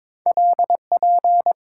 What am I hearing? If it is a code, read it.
Morse code, heard as a keyed tone, about 22 wpm, 700 Hz. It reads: LP
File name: practice / 22wpm / Two Letter-Number Combinations - 22wpm